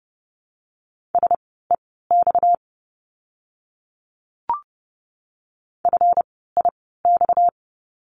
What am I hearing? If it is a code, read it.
Morse code, heard as a keyed tone, about 30 wpm, spaced as for 17 wpm, 700 Hz. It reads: SEX FIX
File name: course / 30:10wpm / 230 Lesson 37f - Letter x - Words 30wpm